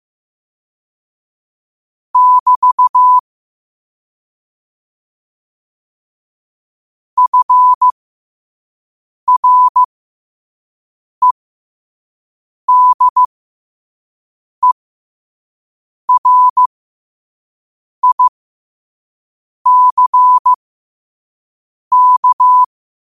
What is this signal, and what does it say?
Morse code, heard as a keyed tone, about 15 wpm, spaced as for 5 wpm, 1000 Hz. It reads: <BT> FREDERICK
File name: qotdfiles / 5wpm